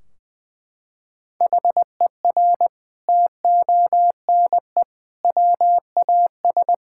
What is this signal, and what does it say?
Morse code, heard as a keyed tone, about 20 wpm, 700 Hz. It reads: HER TONE WAS
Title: Sentences from Top 1600 Words 20wpm